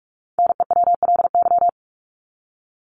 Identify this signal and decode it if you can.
Morse code, heard as a keyed tone, about 44 wpm, 700 Hz. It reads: NEWLY
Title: LY Suffix - Top 100 Words with Base Words 45wpm